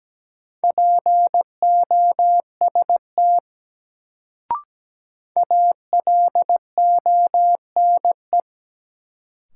Morse code, heard as a keyed tone, about 17 wpm, 700 Hz. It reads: POST ALONE
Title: Sentences from Top 1000 Words - New Word Drill - 17wpm